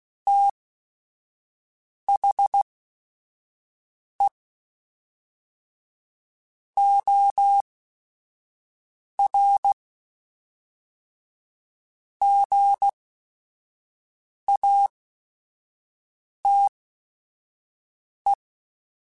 Morse code, heard as a keyed tone, about 16 wpm, spaced as for 5 wpm, 800 Hz. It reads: THE OR GATE
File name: Technical Subjects / Technical Subjects-mp3s / OR Gate @5